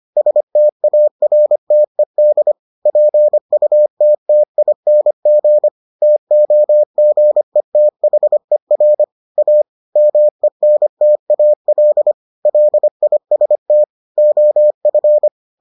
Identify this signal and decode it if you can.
Morse code, heard as a keyed tone, about 25 wpm, 600 Hz. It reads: STARTED PUTTING TOGETHER A MENTAL LIST OF